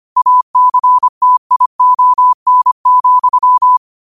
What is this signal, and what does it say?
Morse code, heard as a keyed tone, about 25 wpm, 1000 Hz. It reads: ACTION,
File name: qotdfiles / 25wpm